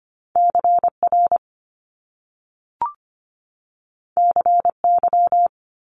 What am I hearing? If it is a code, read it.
Morse code, heard as a keyed tone, about 25 wpm, 700 Hz. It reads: CR CY
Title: Top Two Letter Combinations in Words 25wpm